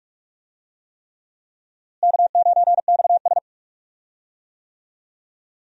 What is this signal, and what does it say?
Morse code, heard as a keyed tone, about 46 wpm, 700 Hz. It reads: K9XS
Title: USA Call Signs - 4 Characters - Repeated 3x - 45wpm